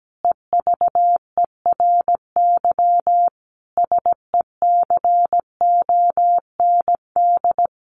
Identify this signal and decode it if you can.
Morse code, heard as a keyed tone, about 17 wpm, 700 Hz. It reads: EVERY SECOND
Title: Sentences from Top 1400 Words 17wpm